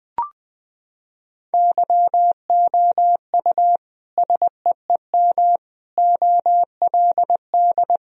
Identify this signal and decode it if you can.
Morse code, heard as a keyed tone, about 21 wpm, 700 Hz. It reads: YOU SEEM OLD
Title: Sentences from Top 1000 Words - Review Part 1 - 20wpm